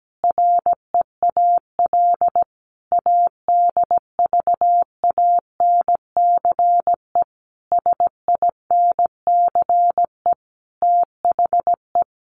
Morse code, heard as a keyed tone, about 17 wpm, 700 Hz. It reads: REAL ADVANCE SINCE THE
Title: Sentences from Top 1400 Words 17wpm